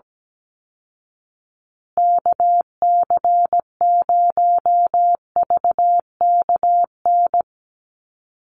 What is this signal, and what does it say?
Morse code, heard as a keyed tone, about 17 wpm, 700 Hz. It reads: KC0VKN
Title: Call Signs Repeated 2x - 17wpm